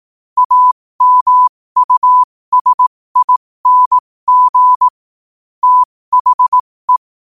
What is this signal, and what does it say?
Morse code, heard as a keyed tone, about 18 wpm, spaced as for 15 wpm, 1000 Hz. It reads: AMUSING THE
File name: qotdfiles / 15wpm